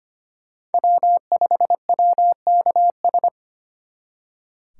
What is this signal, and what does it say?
Morse code, heard as a keyed tone, about 32 wpm, 700 Hz. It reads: W5WKS